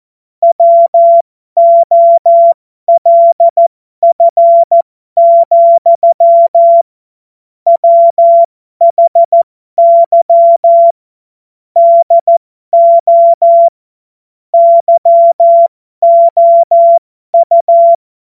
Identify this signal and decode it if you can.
Morse code, heard as a keyed tone, about 14 wpm, spaced as for 12 wpm, 680 Hz. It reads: WOLF, WHY DO YOU
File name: Morsekurs / Witze